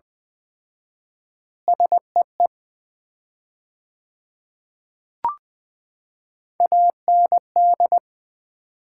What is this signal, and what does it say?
Morse code, heard as a keyed tone, about 20 wpm, 700 Hz. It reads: SEE AND
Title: 4x Word-Spacing - Sets of 3 Words - 20wpm